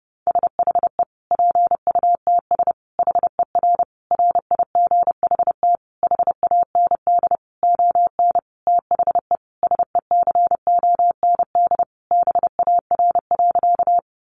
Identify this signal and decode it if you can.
Morse code, heard as a keyed tone, about 30 wpm, 700 Hz. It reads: SHE PUTS HER RIGHT HAND ON THE SECOND BAR.
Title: Easy Phrases 30wpm